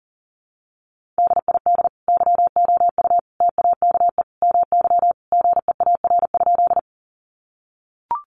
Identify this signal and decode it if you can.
Morse code, heard as a keyed tone, about 40 wpm, 700 Hz. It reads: DID YOU TAKE MY GEAR?
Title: Sentences from Top 1400 Words 40wpm